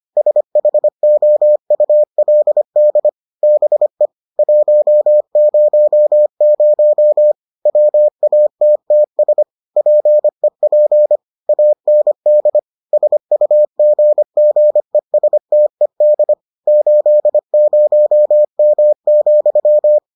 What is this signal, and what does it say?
Morse code, heard as a keyed tone, about 25 wpm, 600 Hz. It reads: SHOULD BE 100 WATTS PEP AND SUGGESTED 80M,